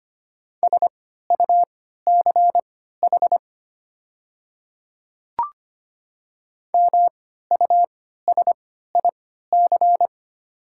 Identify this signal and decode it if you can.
Morse code, heard as a keyed tone, about 25 wpm, spaced as for 14 wpm, 690 Hz. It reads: SUCH MUSIC
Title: 3x Character Spacing - Top 300 Words - 25wpm